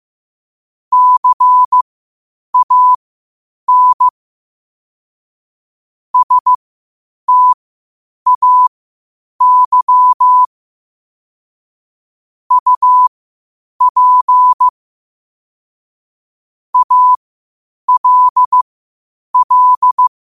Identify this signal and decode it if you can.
Morse code, heard as a keyed tone, about 15 wpm, spaced as for 8 wpm, 1000 Hz. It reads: CAN STAY UP ALL
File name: qotdfiles / 8wpm